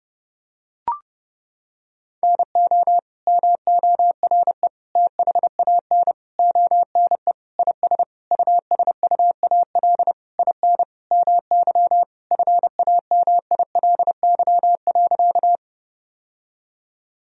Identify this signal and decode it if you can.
Morse code, heard as a keyed tone, about 30 wpm, 700 Hz. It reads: NO MORE THAN ONE IS USUAL IN MY FAMILY.
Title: Sentences from Top 1000 Words - Review Part 2 - 30wpm